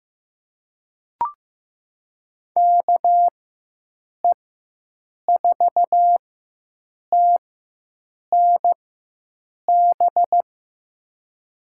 Morse code, heard as a keyed tone, about 15 wpm, spaced as for 7 wpm, 700 Hz. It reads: KE4TNB
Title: USA Call Signs - 4x Character Spacing - 15wpm